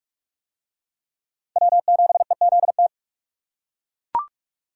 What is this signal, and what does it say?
Morse code, heard as a keyed tone, about 39 wpm, 700 Hz. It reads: W7EZT